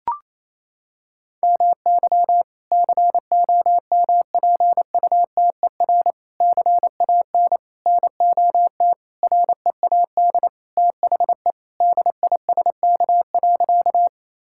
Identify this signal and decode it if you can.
Morse code, heard as a keyed tone, about 28 wpm, 700 Hz. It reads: MY COMPUTER CAN NOT READ THE DISK.
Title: Sentences from Top 1000 Words - Review Part 6 - 28wpm